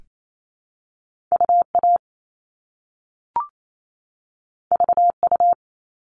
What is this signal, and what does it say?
Morse code, heard as a keyed tone, about 28 wpm, 700 Hz. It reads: UA VU
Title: All Cut Numbers - 2-Digit 28wpm